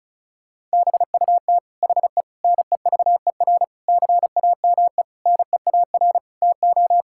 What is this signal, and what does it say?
Morse code, heard as a keyed tone, about 34 wpm, 700 Hz. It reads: BUT HE NEVER CAME NEAR TO
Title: Sentences from Top 1600 Words 35wpm